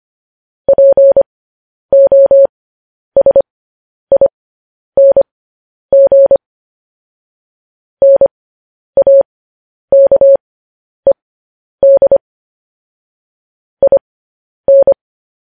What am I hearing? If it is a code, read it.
Morse code, heard as a keyed tone, about 25 wpm, spaced as for 10 wpm, 550 Hz. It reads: POSING NAKED IN
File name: podcast / News headlines-1x-25x10